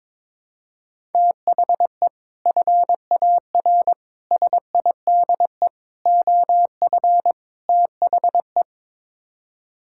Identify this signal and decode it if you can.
Morse code, heard as a keyed tone, about 22 wpm, 700 Hz. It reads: THE FAR SIDE OF THE
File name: practice / 22wpm / Build A Phrase - 5-Word Phrases - 22wpm